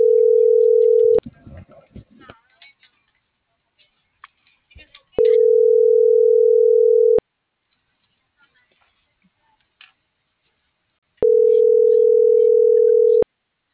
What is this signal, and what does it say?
Ringback tone (North America), 440 + 480 Hz.